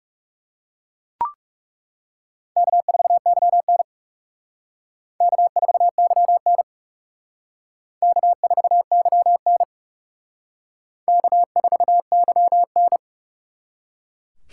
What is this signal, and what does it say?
Morse code, heard as a keyed tone, about 37 wpm, 700 Hz. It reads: K4YN K4YN K4YN K4YN
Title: Speed Racing - USA Call Signs 30wpm